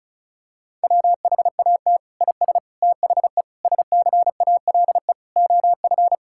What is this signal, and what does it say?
Morse code, heard as a keyed tone, about 34 wpm, 700 Hz. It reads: WHAT IS THE SCALE OF
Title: Sentences from Top 1000 Words - Review Part 5 - 35wpm